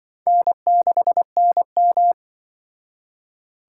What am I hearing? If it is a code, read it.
Morse code, heard as a keyed tone, about 25 wpm, 700 Hz. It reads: N6NM
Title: Speed Racing - USA 1x2 Call Signs 20wpm